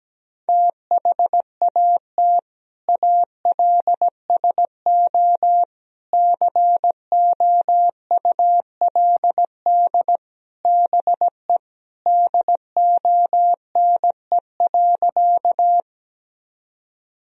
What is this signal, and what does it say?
Morse code, heard as a keyed tone, about 17 wpm, 700 Hz. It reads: THAT ALSO COULD BE DONE.
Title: Sentences from Top 1000 Words - Review Part 1 - 17wpm